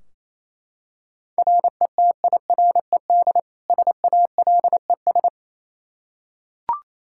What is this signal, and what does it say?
Morse code, heard as a keyed tone, about 28 wpm, 700 Hz. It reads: RETIRED SALES
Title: Practice QSO - Exchange 3 - More Info - 28wpm